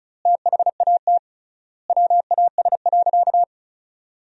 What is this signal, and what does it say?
Morse code, heard as a keyed tone, about 34 wpm, 700 Hz. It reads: THAT WAS.
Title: 3x Word-Spacing - Sentences from Top 300 Words - 35wpm